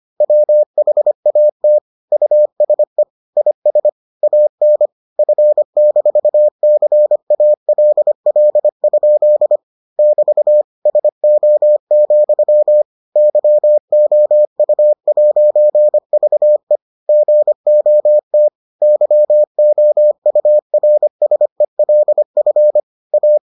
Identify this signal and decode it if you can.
Morse code, heard as a keyed tone, about 25 wpm, 600 Hz. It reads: WHAT USE IS AN F-CALL? <BT> SO, YOU'VE GOT YOURSELF A